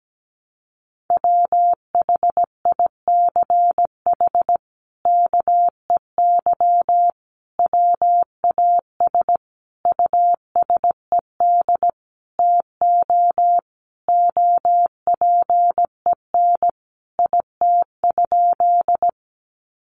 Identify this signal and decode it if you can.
Morse code, heard as a keyed tone, about 17 wpm, 700 Hz. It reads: WHICH KEY WAS USED TO OPEN IT?
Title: Sentences from Top 600 Words 17wpm